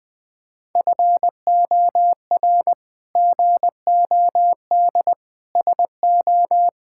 Morse code, heard as a keyed tone, about 20 wpm, 700 Hz. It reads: FOR GOD SO
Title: Sets of 8 Words - 20wpm